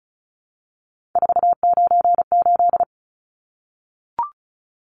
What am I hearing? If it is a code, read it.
Morse code, heard as a keyed tone, about 34 wpm, 700 Hz. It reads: 498
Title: Numbers - 3-Digit 35wpm